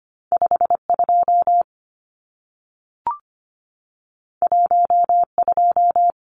Morse code, heard as a keyed tone, about 25 wpm, 700 Hz. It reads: 52 12
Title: Numbers - 2-Digit 25wpm